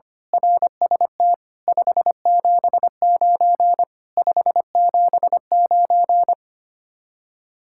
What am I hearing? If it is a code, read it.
Morse code, heard as a keyed tone, about 24 wpm, 700 Hz. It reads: RST 579 579
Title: Speed-Racing Common QSO Phrases - 25wpm